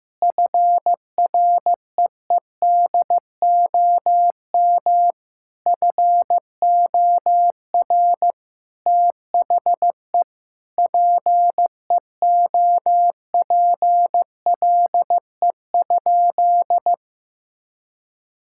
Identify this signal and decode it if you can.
Morse code, heard as a keyed tone, about 15 wpm, 700 Hz. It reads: FREEDOM FOR THE PEOPLE?
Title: Sentences from Top 1000 Words - Review Part 5 - 15wpm